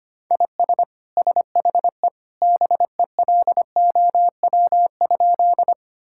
Morse code, heard as a keyed tone, about 25 wpm, 700 Hz. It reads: IS SHE BELOW?
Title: Sentences from Top 1000 Words - Review Part 5 - 25wpm